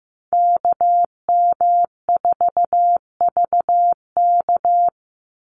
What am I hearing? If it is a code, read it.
Morse code, heard as a keyed tone, about 15 wpm, 700 Hz. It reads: KM4VK
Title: USA Call Signs 15wpm